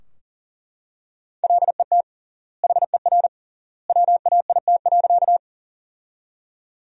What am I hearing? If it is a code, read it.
Morse code, heard as a keyed tone, about 40 wpm, 700 Hz. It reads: LET HER WAIT.
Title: 3x Word-Spacing - Sentences from Top 500 Words - 40wpm